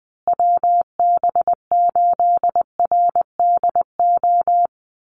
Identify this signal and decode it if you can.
Morse code, heard as a keyed tone, about 20 wpm, 700 Hz. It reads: WB8RDO
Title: USA Call Signs Repeated 2x 20wpm